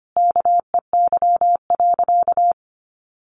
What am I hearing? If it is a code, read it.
Morse code, heard as a keyed tone, about 25 wpm, 700 Hz. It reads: KEY.
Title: Sentences from Top 1300 Words 25wpm